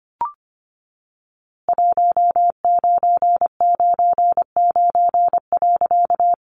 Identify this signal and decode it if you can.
Morse code, heard as a keyed tone, about 25 wpm, 700 Hz. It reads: 1999.
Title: A Sound Of Thunder 25wpm